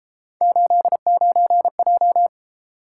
Morse code, heard as a keyed tone, about 30 wpm, 700 Hz. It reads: 89J